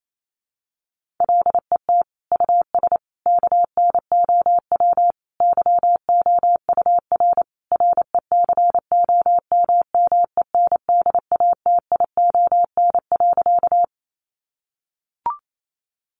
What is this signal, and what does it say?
Morse code, heard as a keyed tone, about 28 wpm, 700 Hz. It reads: LET US KNOW YOUR RECOMMENDATION.
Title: Sentences from Top 1400 Words 28wpm